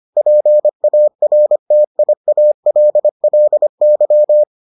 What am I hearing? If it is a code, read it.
Morse code, heard as a keyed tone, about 25 wpm, 600 Hz. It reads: PARTIALLY